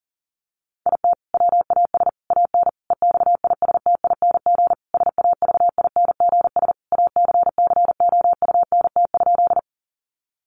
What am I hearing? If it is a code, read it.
Morse code, heard as a keyed tone, about 40 wpm, 700 Hz. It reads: IT WAS AN EXISTING SAVINGS ACCOUNT?
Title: Sentences from Top 1500 Words 40wpm